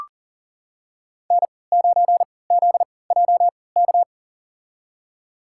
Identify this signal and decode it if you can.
Morse code, heard as a keyed tone, about 38 wpm, spaced as for 22 wpm, 700 Hz. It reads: N9ZJK
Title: USA Call Signs - 5 Characters - 3x Character Spacing - 40wpm